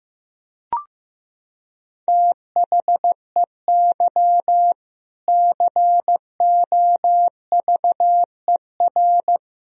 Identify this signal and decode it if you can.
Morse code, heard as a keyed tone, about 15 wpm, 700 Hz. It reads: THEY COVER
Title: Sentences from Top 1000 Words - Review Part 1 - 15wpm